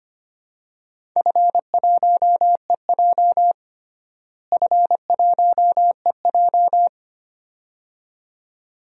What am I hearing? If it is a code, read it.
Morse code, heard as a keyed tone, about 25 wpm, 700 Hz. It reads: F1EJ F1EJ
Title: DXCC-100 Call Signs - 4 Characters - Repeated 2x - 25wpm